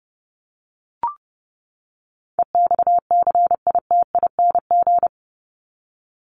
Morse code, heard as a keyed tone, about 28 wpm, 700 Hz. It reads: EXCITING
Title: Top 1751-2000 Words 30wpm